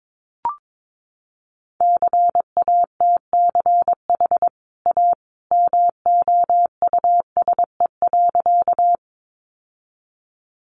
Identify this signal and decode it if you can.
Morse code, heard as a keyed tone, about 22 wpm, 700 Hz. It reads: CATCH A MOUSE.
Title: Sentences from Top 1500 Words 22wpm